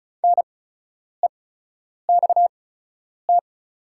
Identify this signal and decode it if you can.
Morse code, heard as a keyed tone, about 34 wpm, spaced as for 10 wpm, 700 Hz. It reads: NEXT
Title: 8x Character Spacing - Top 300 Words - 35wpm